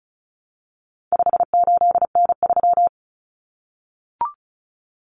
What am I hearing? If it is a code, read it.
Morse code, heard as a keyed tone, about 34 wpm, 700 Hz. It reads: 58N3
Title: Common Cut Numbers - 4-Digit 35wpm